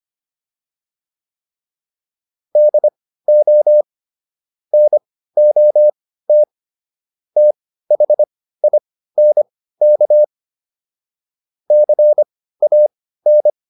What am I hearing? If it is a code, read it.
Morse code, heard as a keyed tone, about 25 wpm, spaced as for 15 wpm, 600 Hz. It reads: DO NOT THINK CAN